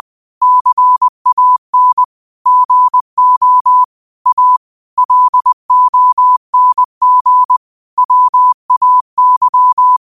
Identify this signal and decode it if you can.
Morse code, heard as a keyed tone, about 20 wpm, 1000 Hz. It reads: CAN GO A LONG WAY